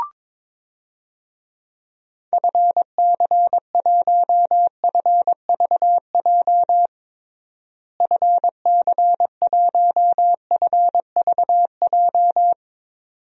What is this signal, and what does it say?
Morse code, heard as a keyed tone, about 22 wpm, 700 Hz. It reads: FC1FVJ FC1FVJ
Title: DXCC-100 Call Signs - 6 Characters - Repeated 2x - 22wpm